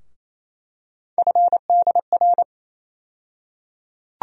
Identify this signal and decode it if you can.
Morse code, heard as a keyed tone, about 28 wpm, 700 Hz. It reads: FDR
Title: USA Call Sign Postfixes - 3 Letters 28wpm